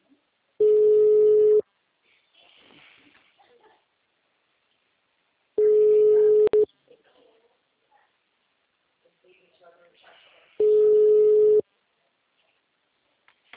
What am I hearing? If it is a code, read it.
Ringback tone (Europe, 425 Hz), with speech.